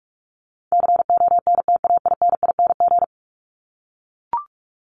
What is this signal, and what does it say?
Morse code, heard as a keyed tone, about 44 wpm, 700 Hz. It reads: CONTAINING
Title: ING Suffix - Top 100 Words with Base Words 45wpm